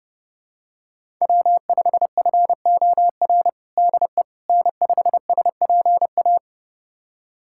Morse code, heard as a keyed tone, about 32 wpm, 700 Hz. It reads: W5FOR DE N5SPA
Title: USA Callsign DE Callsign - 5 Characters - 30wpm